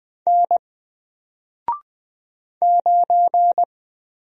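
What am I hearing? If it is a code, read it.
Morse code, heard as a keyed tone, about 20 wpm, 700 Hz. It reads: N 9
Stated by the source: Single letter and number Morse drill at 20 wpm, from N4LCD.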